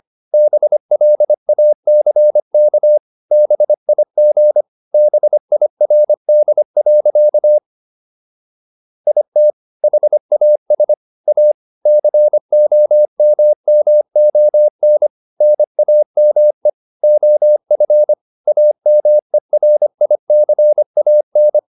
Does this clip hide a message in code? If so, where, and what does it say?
Yes: Morse code from 0.3 to 21.6 s, heard as a keyed tone, about 25 wpm, 600 Hz. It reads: BLACK BIG BIRD. IT HAS A COMMON NAME OF AMERICAN